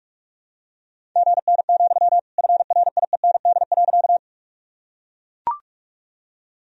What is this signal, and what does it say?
Morse code, heard as a keyed tone, about 41 wpm, 700 Hz. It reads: GN, FRIEND.
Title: Sentences from Top 300 Words and Abbreviations 45wpm